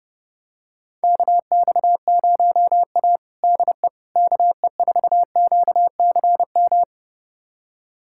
Morse code, heard as a keyed tone, about 30 wpm, 700 Hz. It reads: KX0A DE KE4QCM
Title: USA Callsign DE Callsign 30wpm